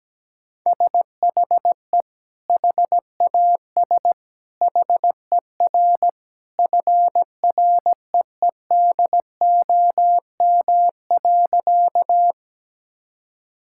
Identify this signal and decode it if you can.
Morse code, heard as a keyed tone, about 17 wpm, 700 Hz. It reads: SHE HAS HER FREEDOM.
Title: Sentences from Top 1000 Words - Review Part 5 - 17wpm